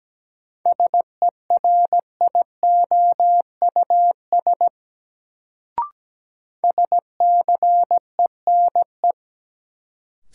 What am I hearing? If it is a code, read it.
Morse code, heard as a keyed tone, about 17 wpm, 700 Hz. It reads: SERIOUS SCENE
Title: Sentences from Top 1100 Words - New Word Drill - 17wpm